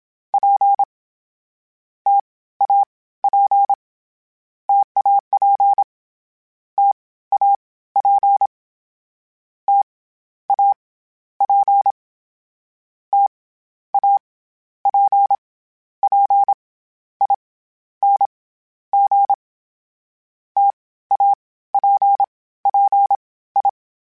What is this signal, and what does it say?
Morse code, heard as a keyed tone, about 27 wpm, 800 Hz. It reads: P T A P TAP T A P T A P T A P P I N G T A P P I
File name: Variable Speed / Variable Speed-mp3s / T AOEI's sp @25